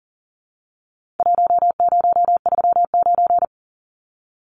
Morse code, heard as a keyed tone, about 40 wpm, 700 Hz. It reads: 1039
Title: Numbers - Contest Style - 4-Digit 40wpm